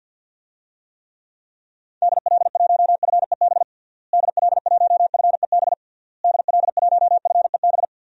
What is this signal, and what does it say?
Morse code, heard as a keyed tone, about 51 wpm, 700 Hz. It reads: DL1FEB DL1FEB DL1FEB
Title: DXCC-100 Call Signs - 6 Characters - Repeated 3x - 50wpm